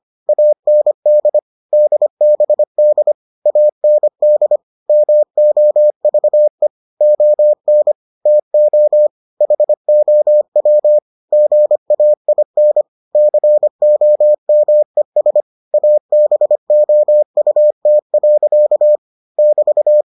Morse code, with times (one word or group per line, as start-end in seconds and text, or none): 0.3-1.4 AND
1.7-3.1 DBD
3.5-4.6 AND
4.9-6.7 MOVE
7.0-7.9 ON
8.3-9.1 TO
9.4-11.0 HOW
11.3-12.8 GAIN
13.2-15.4 COMES
15.7-19.0 ABOUT.
19.4-20.0 <BT>